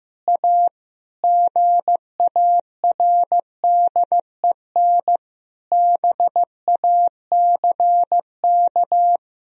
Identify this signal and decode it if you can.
Morse code, heard as a keyed tone, about 15 wpm, 700 Hz. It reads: A GARDEN BACK
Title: Sentences from Top 1000 Words - Review Part 4 - 15wpm